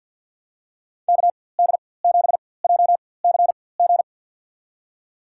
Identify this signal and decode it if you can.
Morse code, heard as a keyed tone, about 50 wpm, spaced as for 23 wpm, 700 Hz. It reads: KD7JCG
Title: USA Call Signs - 6 Characters - 4x Character Spacing - 50wpm